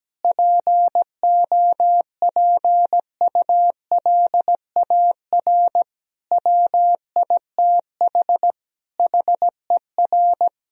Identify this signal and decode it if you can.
Morse code, heard as a keyed tone, about 17 wpm, 700 Hz. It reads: POPULAR WITH HER
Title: Sentences from Top 1000 Words - Review Part 3 - 17wpm